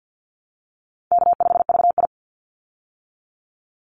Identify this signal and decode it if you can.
Morse code, heard as a keyed tone, about 50 wpm, 700 Hz. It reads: K5VI
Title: Easy Call Signs Repeated 2x - 50wpm